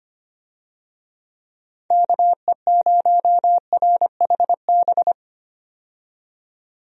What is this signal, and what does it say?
Morse code, heard as a keyed tone, about 25 wpm, 700 Hz. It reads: KE0RHB
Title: USA Call Signs - 6 Characters - 25wpm